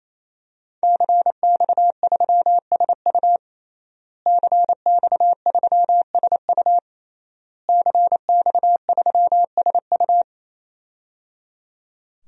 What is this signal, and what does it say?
Morse code, heard as a keyed tone, about 28 wpm, 700 Hz. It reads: CX3SU CX3SU CX3SU
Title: DXCC-100 Call Signs - 5 Characters - Repeated 3x - 28wpm